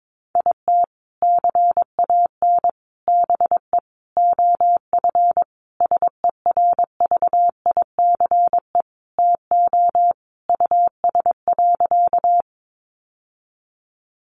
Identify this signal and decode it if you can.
Morse code, heard as a keyed tone, about 22 wpm, 700 Hz. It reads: IT CAN BE OF SERVICE TO US.
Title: Sentences from Top 1200 Words 22wpm